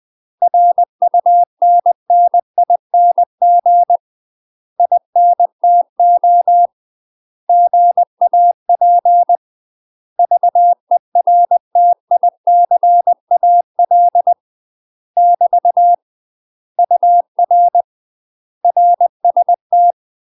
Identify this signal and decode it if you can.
Morse code, heard as a keyed tone, about 20 wpm, 700 Hz. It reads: RUNNING INTO GAP VERTICAL <BT> UR RST
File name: UK MASTER QSOs TRAINING / 20WPM / UK MASTER QSOs 0035